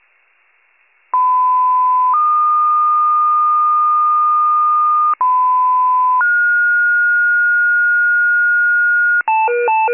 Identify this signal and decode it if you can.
Two-tone page, heard as a one-second tone, then a three-second tone, about 1010 then 1220 Hz, about 1010 then 1430 Hz, with noise. It is heard from 1.1 to 9.2 s.